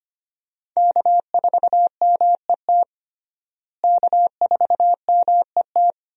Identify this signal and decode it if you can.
Morse code, heard as a keyed tone, about 25 wpm, 700 Hz. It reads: K4MET K4MET
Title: USA Call Signs - 5 Characters - Repeated 2x - 25wpm